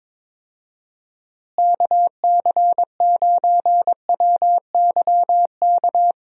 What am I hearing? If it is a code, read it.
Morse code, heard as a keyed tone, about 22 wpm, 700 Hz. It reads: KC9WYK
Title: USA Call Signs - 6 Characters - Repeated 2x - 22wpm